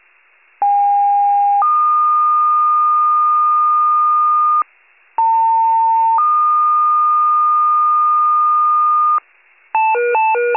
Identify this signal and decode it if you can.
Two-tone page, heard as a one-second tone, then a three-second tone, about 810 then 1220 Hz, about 900 then 1220 Hz. It is heard from 0.6 to 9.2 s.
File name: pager / 73 RESCUE / wcalert